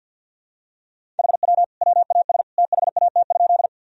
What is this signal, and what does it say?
Morse code, heard as a keyed tone, about 46 wpm, 700 Hz. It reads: HW WAS THAT?
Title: Sentences from Top 100 Words and Abbreviations 50wpm